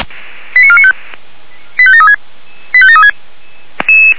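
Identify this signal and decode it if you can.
Five-tone selective call (ZVEI), heard as quick runs of five tones, tones of about 70 ms. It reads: 89495 85736 85736
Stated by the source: ZVEI five-tone call received on SDR.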